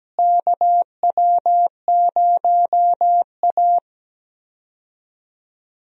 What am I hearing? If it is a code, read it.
Morse code, heard as a keyed tone, about 17 wpm, 700 Hz. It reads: KW0A
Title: USA Call Signs - 4 Characters - Repeated 3x - 17wpm